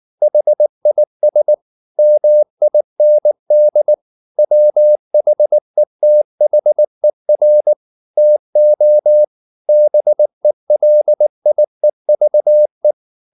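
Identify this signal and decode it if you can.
Morse code, heard as a keyed tone, about 19 wpm, 600 Hz. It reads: HIS MIND WHETHER TO BELIEVE
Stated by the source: Morse code at 19 wpm from CW med Gnister, reading Ken Follett's novel Never in English.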